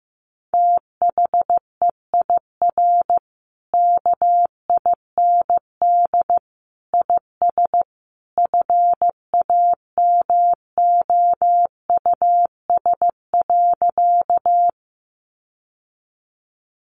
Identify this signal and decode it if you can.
Morse code, heard as a keyed tone, about 15 wpm, 700 Hz. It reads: THEIR KIND IS FAMOUS.
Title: Sentences from Top 1500 Words 15wpm